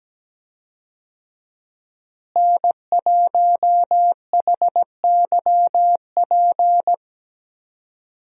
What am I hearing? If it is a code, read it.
Morse code, heard as a keyed tone, about 17 wpm, 700 Hz. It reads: N1HYP
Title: USA Call Signs - 5 Characters - Repeated 2x - 17wpm